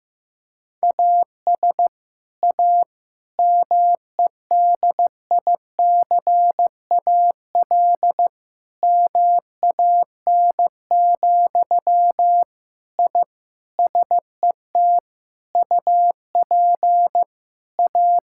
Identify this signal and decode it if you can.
Morse code, heard as a keyed tone, about 16 wpm, 700 Hz. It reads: AS A MEDICAL MAN, I SET UP A
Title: Sentences from Top 1000 Words - Review Part 5 - 15wpm